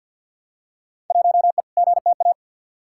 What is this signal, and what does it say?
Morse code, heard as a keyed tone, about 51 wpm, 700 Hz. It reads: 1E GTA
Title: Field Day Exchanges - 50wpm